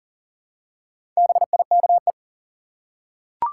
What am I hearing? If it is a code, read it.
Morse code, heard as a keyed tone, about 40 wpm, 700 Hz. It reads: BIKE
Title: Four Letter Words - 40wpm